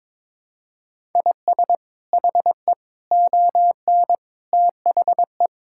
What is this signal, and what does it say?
Morse code, heard as a keyed tone, about 22 wpm, 700 Hz. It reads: IS HE ON THE
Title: Sentences from Top 1000 Words - Review Part 6 - 22wpm